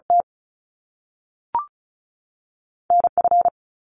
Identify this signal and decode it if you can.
Morse code, heard as a keyed tone, about 36 wpm, spaced as for 6 wpm, 700 Hz. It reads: TNF
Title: Top Two Letter Combinations in Words 35wpm